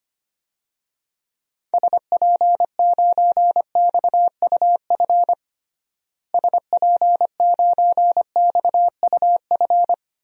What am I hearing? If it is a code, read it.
Morse code, heard as a keyed tone, about 25 wpm, 700 Hz. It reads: SP9XUF SP9XUF
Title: DXCC-100 Call Signs - 6 Characters - Repeated 2x - 25wpm